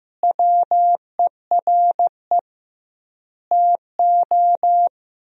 Morse code, heard as a keyed tone, about 15 wpm, 700 Hz. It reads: WERE TO
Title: 2x Word-Spacing - Sentences from Top 400 Words - 15wpm